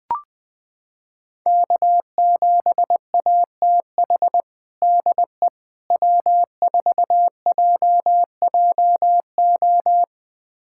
Morse code, heard as a keyed tone, about 20 wpm, 700 Hz. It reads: K7ATH DE W4JJO
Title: USA Callsign DE Callsign - 5 Characters - 20wpm